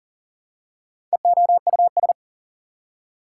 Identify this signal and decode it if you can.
Morse code, heard as a keyed tone, about 40 wpm, 700 Hz. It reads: EOUS
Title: Prefixes and Suffixes 40wpm